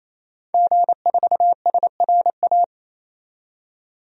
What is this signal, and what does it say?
Morse code, heard as a keyed tone, about 28 wpm, 700 Hz. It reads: G4SRA
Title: DXCC-100 Call Signs - 5 Characters - 28wpm